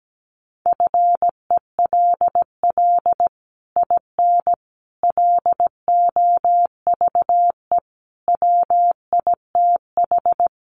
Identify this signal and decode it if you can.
Morse code, heard as a keyed tone, about 17 wpm, 700 Hz. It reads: FELL IN LOVE WITH
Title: Sets of 5 Words Encore2 17wpm